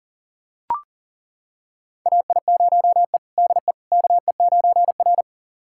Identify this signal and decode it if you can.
Morse code, heard as a keyed tone, about 40 wpm, 700 Hz. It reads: AI0E DE KE9R
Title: USA Callsign DE Callsign - 4 Characters - 40wpm